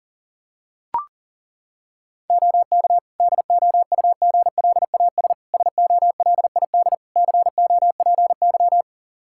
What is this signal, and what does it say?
Morse code, heard as a keyed tone, about 40 wpm, 700 Hz. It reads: OK DOUGLAS SOLID COPY
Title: Practice QSO - Exchange 3 - More Info - 40wpm